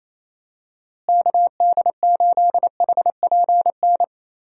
Morse code, heard as a keyed tone, about 28 wpm, 700 Hz. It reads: KD8HPN
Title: USA Call Signs - 6 Characters - Repeated 3x - 28wpm